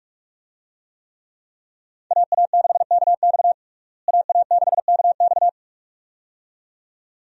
Morse code, heard as a keyed tone, about 46 wpm, 700 Hz. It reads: AA6KX AA6KX
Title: USA Call Signs - 5 Characters - Repeated 2x - 45wpm